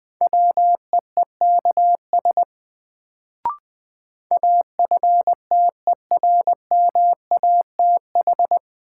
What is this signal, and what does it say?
Morse code, heard as a keyed tone, about 20 wpm, 700 Hz. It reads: WEEKS AFTERMATH
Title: Sets of 5 Words - New Word Drill - 20wpm